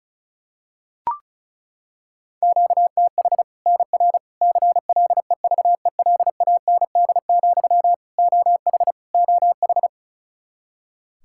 Morse code, heard as a keyed tone, about 34 wpm, 700 Hz. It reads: QTH NR CLEVELAND, OH OH
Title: Practice QSO - Exchange 1 - Beginning - 35wpm